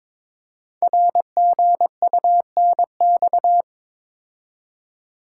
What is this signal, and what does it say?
Morse code, heard as a keyed tone, about 23 wpm, 700 Hz. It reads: RGUNX
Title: Code Groups - 5 Letters - 22wpm